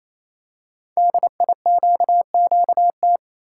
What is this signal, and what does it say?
Morse code, heard as a keyed tone, about 29 wpm, 700 Hz. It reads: DIQQT
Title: Code Groups - 5 Letters - 28wpm